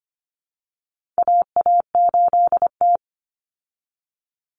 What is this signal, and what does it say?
Morse code, heard as a keyed tone, about 25 wpm, 700 Hz. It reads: AA8T
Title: Common Cut Numbers - 4-Digit 25wpm